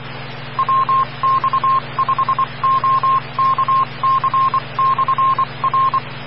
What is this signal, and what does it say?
Morse code, heard as a keyed tone, about 24 wpm, 1060 Hz, with noise. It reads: WX5OKC/R
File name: RR Skywarn / WX5OKC Amateur Repeater Hz